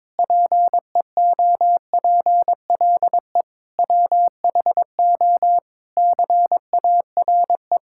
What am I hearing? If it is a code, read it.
Morse code, heard as a keyed tone, about 22 wpm, 700 Hz. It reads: PEOPLE WHO CARE
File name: Sentences with 7-Word Phrases - 22wpm